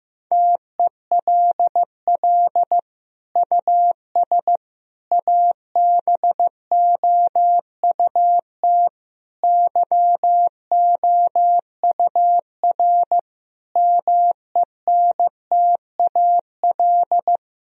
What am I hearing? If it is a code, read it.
Morse code, heard as a keyed tone, about 16 wpm, 700 Hz. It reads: TELL US ABOUT YOUR MENTAL
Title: Sentences from Top 1000 Words - Review Part 4 - 15wpm